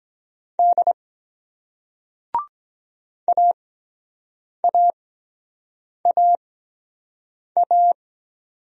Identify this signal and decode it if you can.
Morse code, heard as a keyed tone, about 23 wpm, spaced as for 7 wpm, 700 Hz. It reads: D AAAA